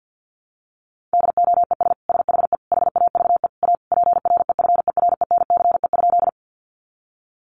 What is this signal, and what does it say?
Morse code, heard as a keyed tone, about 50 wpm, 700 Hz. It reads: DOES SHE HAVE A PREFERENCE?
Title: Sentences from Top 1500 Words 50wpm